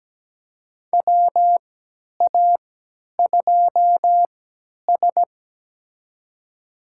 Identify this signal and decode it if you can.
Morse code, heard as a keyed tone, about 17 wpm, spaced as for 10 wpm, 700 Hz. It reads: WA2S
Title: USA Call Signs - 4 Characters - 3x Character Spacing - 17wpm